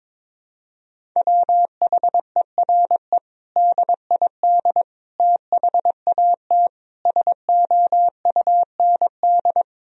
Morse code, heard as a keyed tone, about 21 wpm, 700 Hz. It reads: WHERE DID THAT SOUND
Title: Sentences from Top 1000 Words - Review Part 1 - 22wpm